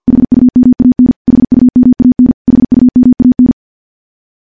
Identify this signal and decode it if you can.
Morse code, heard as a keyed tone, about 46 wpm, 250 Hz. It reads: HUMAN HUMAN HUMAN
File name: morse / morse-training-50WPM